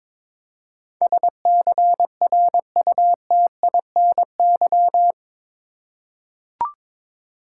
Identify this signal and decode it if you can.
Morse code, heard as a keyed tone, about 22 wpm, 700 Hz. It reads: SCRUTINY
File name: practice / 22wpm / Eight Letter Words - 22wpm